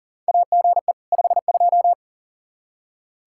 Morse code, heard as a keyed tone, about 37 wpm, 700 Hz. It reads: AGE 52
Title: Practice QSO - Exchange 3 - More Info - 40wpm